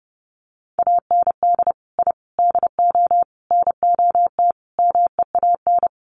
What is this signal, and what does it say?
Morse code, heard as a keyed tone, about 30 wpm, 700 Hz. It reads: AND I DO NOT MEAN
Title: Sets of 5 Words Encore2 30wpm